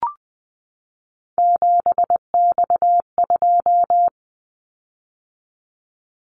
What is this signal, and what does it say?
Morse code, heard as a keyed tone, about 20 wpm, 700 Hz. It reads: 7X2
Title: Callsign Prefixes for Top 100 DXCC - 20wpm